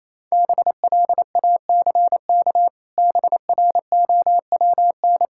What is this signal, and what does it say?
Morse code, heard as a keyed tone, about 28 wpm, 700 Hz. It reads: BLACK BROWN
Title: Pangram Challenge v2 - 28wpm